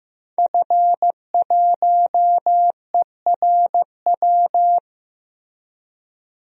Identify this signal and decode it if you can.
Morse code, heard as a keyed tone, about 15 wpm, 700 Hz. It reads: F1ERW
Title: DXCC-100 Call Signs - 5 Characters - Repeated 2x - 15wpm